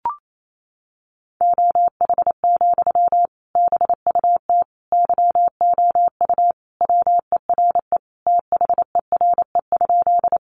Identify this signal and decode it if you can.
Morse code, heard as a keyed tone, about 28 wpm, 700 Hz. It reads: OH, BUT YOU WERE THERE?
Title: Sentences from Top 500 Words 28wpm